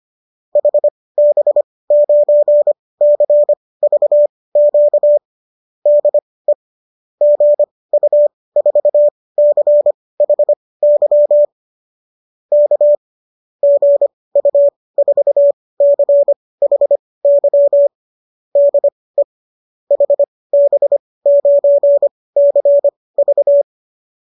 Morse code, heard as a keyed tone, about 25 wpm, spaced as for 18 wpm, 570 Hz. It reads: HB9CVQ DE GU4CHY K GU4CHY DE HB9CV